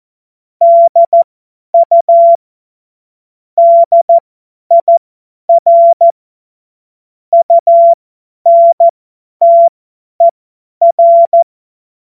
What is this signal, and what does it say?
Morse code, heard as a keyed tone, about 14 wpm, spaced as for 10 wpm, 680 Hz. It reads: DU DIR UNTER
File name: Morsekurs / Witze / Witz 26. Oktober 2021, 50 F70